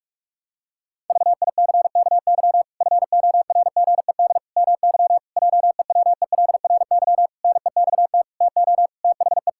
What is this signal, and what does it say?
Morse code, heard as a keyed tone, about 41 wpm, 700 Hz. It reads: VICKY FORGED MY JEWELRY NEXT TO THE
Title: Pangram Challenge v2 - 45wpm